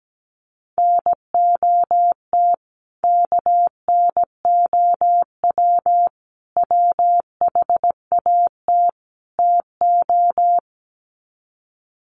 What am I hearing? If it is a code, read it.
Morse code, heard as a keyed tone, about 17 wpm, 700 Hz. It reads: NOT KNOW WHAT TO
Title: Sets of 4 Words 17wpm